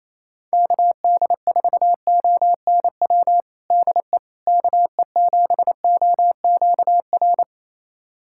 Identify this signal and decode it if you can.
Morse code, heard as a keyed tone, about 28 wpm, 700 Hz. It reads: KD4ONW DE KE7OQR
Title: USA Callsign DE Callsign - 6 Characters - 28wpm